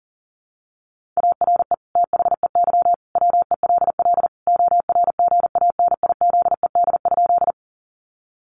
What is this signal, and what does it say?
Morse code, heard as a keyed tone, about 40 wpm, 700 Hz. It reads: ARE THEY WELL ORGANIZED?
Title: Sentences from Top 1300 Words 40wpm